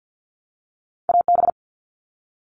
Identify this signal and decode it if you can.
Morse code, heard as a keyed tone, about 50 wpm, 700 Hz. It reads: AB
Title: ARRL Sections 50wpm